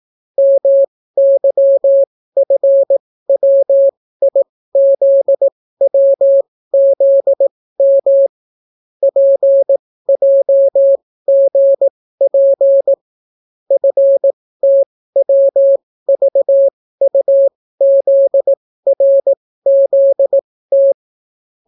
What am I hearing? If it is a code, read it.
Morse code, heard as a keyed tone, about 18 wpm, spaced as for 14 wpm, 550 Hz. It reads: MYFWIZWZM PJGP FTWVUZRZT